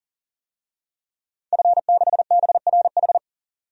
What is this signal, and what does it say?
Morse code, heard as a keyed tone, about 40 wpm, 700 Hz. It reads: F6BRH
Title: DXCC-100 Call Signs - 5 Characters - 40wpm